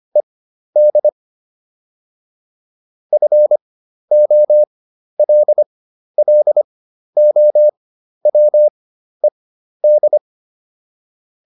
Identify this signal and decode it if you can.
Morse code, heard as a keyed tone, about 25 wpm, spaced as for 12 wpm, 600 Hz. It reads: ED FOLLOWED